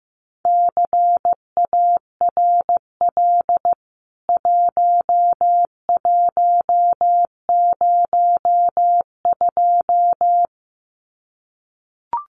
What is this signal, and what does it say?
Morse code, heard as a keyed tone, about 15 wpm, 700 Hz. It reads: CARL 1102
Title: CWT Exchange 15wpm